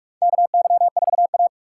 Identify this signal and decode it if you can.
Morse code, heard as a keyed tone, about 44 wpm, 700 Hz. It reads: KY4A
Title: USA Call Signs - 4 Characters - 2x Character Spacing - 45wpm